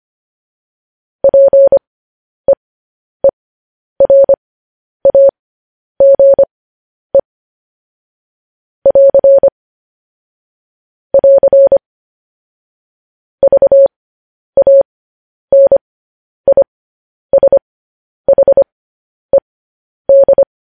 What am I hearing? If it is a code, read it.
Morse code, heard as a keyed tone, about 25 wpm, spaced as for 10 wpm, 550 Hz. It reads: PEERAGE <AR> <AR> VANISHED
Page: news-headlines-in-morse-code-at-10-wpm